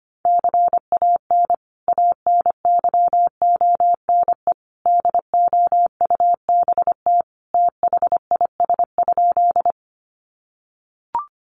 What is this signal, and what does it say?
Morse code, heard as a keyed tone, about 25 wpm, 700 Hz. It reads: CAN ANYONE DOUBT THIS?
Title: Sentences from Top 1500 Words 25wpm